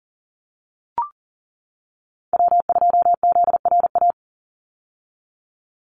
Morse code, heard as a keyed tone, about 40 wpm, 700 Hz. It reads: W2ZRA
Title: Call Signs - 40wpm